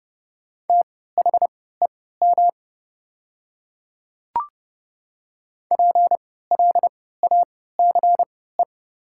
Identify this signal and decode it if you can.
Morse code, heard as a keyed tone, about 30 wpm, spaced as for 17 wpm, 700 Hz. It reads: THEM PLACE
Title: 3x Character Spacing - Top 300 Words - 30wpm